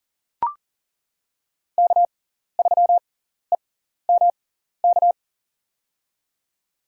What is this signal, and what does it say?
Morse code, heard as a keyed tone, about 40 wpm, spaced as for 14 wpm, 700 Hz. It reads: K3EMK
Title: USA Call Signs - 5 Characters - 6x Character Spacing - 40wpm